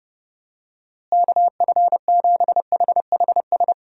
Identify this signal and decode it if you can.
Morse code, heard as a keyed tone, about 31 wpm, 700 Hz. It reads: KF7HHS
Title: USA Call Signs - 6 Characters - Repeated 2x - 30wpm